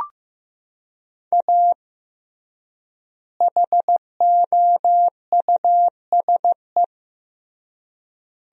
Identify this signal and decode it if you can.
Morse code, heard as a keyed tone, about 15 wpm, 700 Hz. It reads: A HOUSE
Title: 3x Word-Spacing - Sentences from Top 400 Words - 15wpm